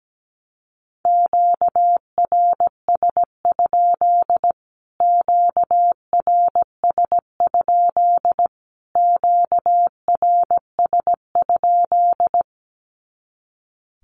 Morse code, heard as a keyed tone, about 17 wpm, 700 Hz. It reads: QRS? QRS? QRS?
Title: Q-Codes Repeated 3x 17wpm